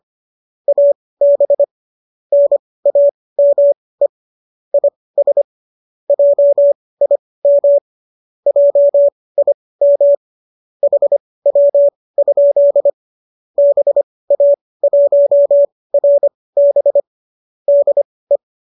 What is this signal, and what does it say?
Morse code, heard as a keyed tone, about 25 wpm, spaced as for 18 wpm, 570 Hz. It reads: AB NAME IS JIM JIM HW? BA1RB DE